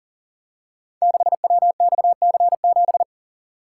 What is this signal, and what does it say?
Morse code, heard as a keyed tone, about 38 wpm, 690 Hz. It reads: 6WXC7
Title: Code Groups - 5 Letter:Numbers - 40wpm